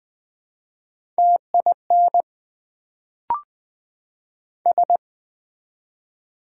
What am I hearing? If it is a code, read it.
Morse code, heard as a keyed tone, about 20 wpm, 700 Hz. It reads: TIN S
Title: Build A Word - 1 Letter At A Time - 3-Letter Words - 20wpm